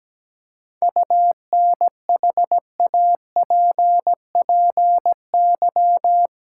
Morse code, heard as a keyed tone, about 17 wpm, 700 Hz. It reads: UNHAPPY
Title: Seven Letter Words - 17wpm